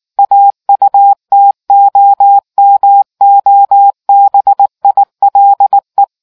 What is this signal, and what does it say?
Morse code, heard as a keyed tone, about 19 wpm, 800 Hz. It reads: AUTOMOBILE